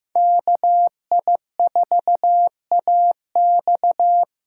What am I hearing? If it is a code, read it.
Morse code, heard as a keyed tone, about 15 wpm, 700 Hz. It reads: KI4AX
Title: USA Call Signs - 6 Characters - 6x Character Spacing - 15wpm